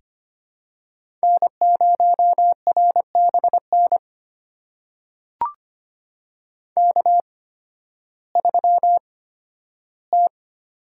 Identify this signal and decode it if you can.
Morse code, heard as a keyed tone, about 25 wpm, spaced as for 13 wpm, 700 Hz. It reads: N0RBN K3T
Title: USA Call Signs - 8x Character Spacing - 25wpm